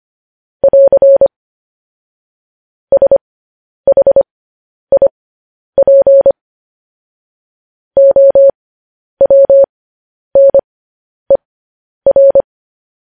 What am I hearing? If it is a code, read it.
Morse code, heard as a keyed tone, about 25 wpm, spaced as for 10 wpm, 550 Hz. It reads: <AR> SHIP OWNER
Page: news headlines in morse code at 10 wpm